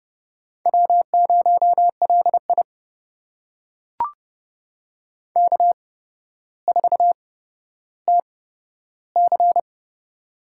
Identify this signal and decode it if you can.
Morse code, heard as a keyed tone, about 28 wpm, spaced as for 13 wpm, 700 Hz. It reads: W0LI K4TC
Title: USA Call Signs - 4 Characters - 8x Character Spacing - 30wpm